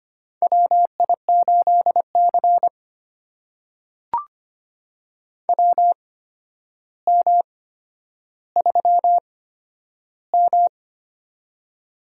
Morse code, heard as a keyed tone, about 25 wpm, spaced as for 11 wpm, 700 Hz. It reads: WI8C WM3M
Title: USA Call Signs - 4 Characters - 8x Character Spacing - 25wpm